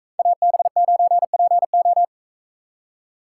Morse code, heard as a keyed tone, about 44 wpm, 700 Hz. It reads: AB9PO